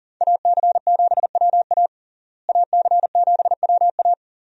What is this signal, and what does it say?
Morse code, heard as a keyed tone, about 40 wpm, 700 Hz. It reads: AC7WA AC7WA
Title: USA Call Signs - 5 Characters - Repeated 2x - 40wpm